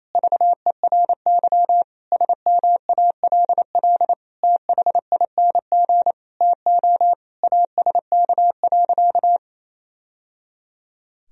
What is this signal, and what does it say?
Morse code, heard as a keyed tone, about 28 wpm, 700 Hz. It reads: VERY SMALL THING TO ASK.
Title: Sentences from Top 1000 Words - Review Part 1 - 28wpm